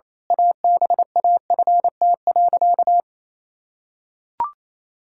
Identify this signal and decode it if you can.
Morse code, heard as a keyed tone, about 28 wpm, 700 Hz. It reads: ABAFT.
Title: Pangram Challenge v2 - 28wpm